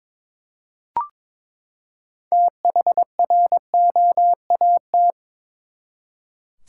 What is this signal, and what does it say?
Morse code, heard as a keyed tone, about 22 wpm, 700 Hz. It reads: THROAT
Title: Six Letter Words - 22wpm